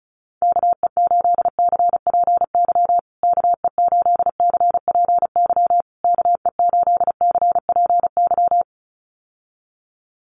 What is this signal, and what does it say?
Morse code, heard as a keyed tone, about 34 wpm, 700 Hz. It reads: KE8CPY KE8CPY KE8CPY
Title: USA Call Signs Repeated 3x 35wpm